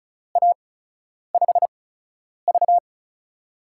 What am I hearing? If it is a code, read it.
Morse code, heard as a keyed tone, about 36 wpm, 700 Hz. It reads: A 5 V
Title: USA Call Signs - 4 Characters - 8x Character Spacing - 35wpm